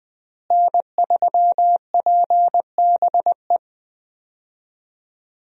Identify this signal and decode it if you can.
Morse code, heard as a keyed tone, about 20 wpm, 700 Hz. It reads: N3PBE
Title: USA Call Signs - 5 Characters - Repeated 3x - 20wpm